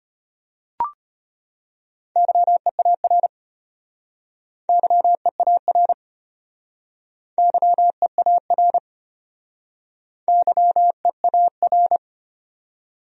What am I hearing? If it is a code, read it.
Morse code, heard as a keyed tone, about 33 wpm, 700 Hz. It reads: YEAR YEAR YEAR YEAR